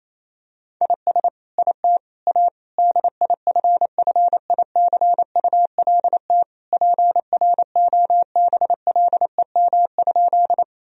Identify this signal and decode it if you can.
Morse code, heard as a keyed tone, about 28 wpm, 700 Hz. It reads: IS IT A DIFFICULT PROBLEM?
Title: Sentences from Top 1000 Words - Review Part 2 - 28wpm